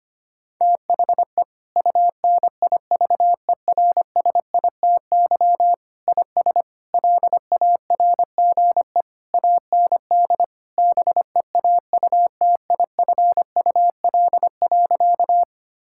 Morse code, heard as a keyed tone, about 25 wpm, 700 Hz. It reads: THE UNIVERSITY IS LARGE AND BEAUTIFUL.
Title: Sentences from Top 1000 Words - Review Part 5 - 25wpm